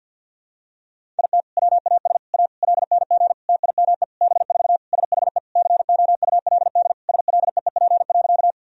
Morse code, heard as a keyed tone, about 50 wpm, 700 Hz. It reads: IT WAS A LNG TIME B4 SHE COULD SLEEP.
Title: Sentences from Top 500 Words and Abbreviations 50wpm